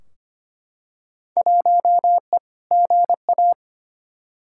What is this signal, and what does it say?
Morse code, heard as a keyed tone, about 25 wpm, 700 Hz. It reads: 1E GA
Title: Field Day Exchanges - 25wpm